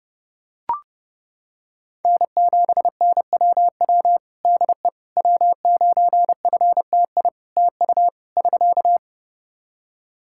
Morse code, heard as a keyed tone, about 30 wpm, 700 Hz. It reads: N7NWW DE W9FTI TU <SK>
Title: Practice QSO - Exchange 4 - Ending - 30wpm